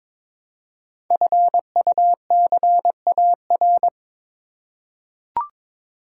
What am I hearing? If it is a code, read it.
Morse code, heard as a keyed tone, about 22 wpm, 700 Hz. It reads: FUCAR
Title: Code Groups - 5 Letter:Numbers - 22wpm